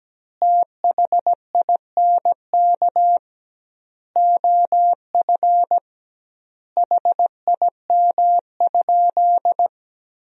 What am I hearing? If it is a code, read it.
Morse code, heard as a keyed tone, about 17 wpm, 700 Hz. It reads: THINK OF HIM?
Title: 2x Word-Spacing - Sentences from Top 200 Words - 17wpm